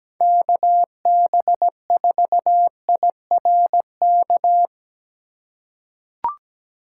Morse code, heard as a keyed tone, about 17 wpm, 700 Hz. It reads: KB4IRK
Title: USA Call Signs - 6 Characters - 4x Character Spacing - 17wpm